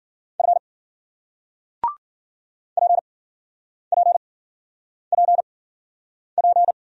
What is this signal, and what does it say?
Morse code, heard as a keyed tone, about 53 wpm, spaced as for 9 wpm, 700 Hz. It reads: F PPPP